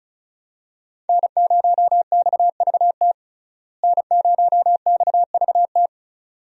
Morse code, heard as a keyed tone, about 36 wpm, 700 Hz. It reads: N0XVT N0XVT
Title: USA Call Signs - 5 Characters - Repeated 2x - 35wpm